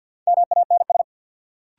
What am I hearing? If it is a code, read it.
Morse code, heard as a keyed tone, about 47 wpm, 700 Hz. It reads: MANS
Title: Sentences from Top 1000 Words - Review Part 6 - New Word Drill - 50wpm